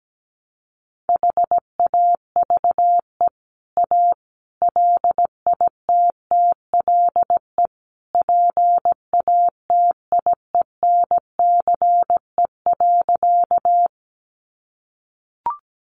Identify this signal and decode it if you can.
Morse code, heard as a keyed tone, about 17 wpm, 700 Hz. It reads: HAVE A LITTLE PATIENCE.
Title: Sentences from Top 1300 Words 17wpm